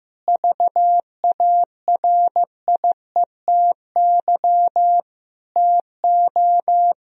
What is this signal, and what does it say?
Morse code, heard as a keyed tone, about 15 wpm, 700 Hz. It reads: VARIETY TO
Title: Sentences from Top 1000 Words - Review Part 4 - 15wpm